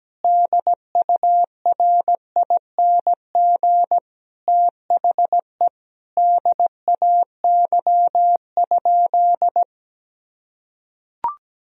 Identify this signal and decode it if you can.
Morse code, heard as a keyed tone, about 17 wpm, 700 Hz. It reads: DURING THE DAY?
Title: Sentences from Top 1000 Words - Review Part 4 - 17wpm